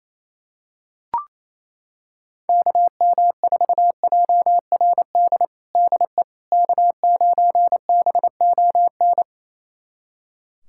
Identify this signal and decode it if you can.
Morse code, heard as a keyed tone, about 28 wpm, 700 Hz. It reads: KM4JRD DE K9BON
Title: USA Callsign DE Callsign 28wpm